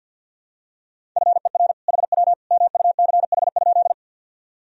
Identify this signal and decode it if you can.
Morse code, heard as a keyed tone, about 50 wpm, 700 Hz. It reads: FER HW MUCH?
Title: Sentences from Top 200 Words and Abbreviations 50wpm